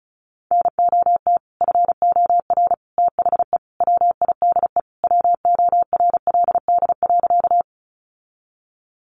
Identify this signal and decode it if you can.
Morse code, heard as a keyed tone, about 34 wpm, 700 Hz. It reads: NOT FOR THE WIDE WORLD.
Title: Sentences from Top 900 Words 35wpm